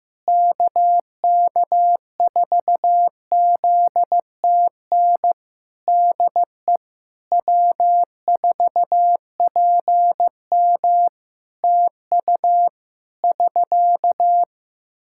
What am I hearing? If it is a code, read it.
Morse code, heard as a keyed tone, about 15 wpm, 700 Hz. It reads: KK4ZTN DE W4PM TU <SK>
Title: Practice QSO - Exchange 4 - Ending - 15wpm